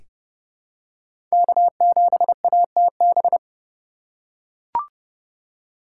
Morse code, heard as a keyed tone, about 30 wpm, 700 Hz. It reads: K7ATB
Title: USA Call Signs - 5 Characters - 4x Character Spacing - 30wpm